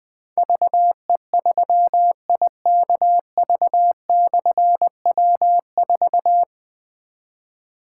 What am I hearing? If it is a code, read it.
Morse code, heard as a keyed tone, about 20 wpm, 700 Hz. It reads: VE3IKV/W4
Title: Difficult Call Signs 20wpm